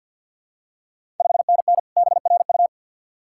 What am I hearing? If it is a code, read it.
Morse code, heard as a keyed tone, about 52 wpm, 690 Hz. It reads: 5NN BRU
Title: Ontario QSO Party Exchange 50wpm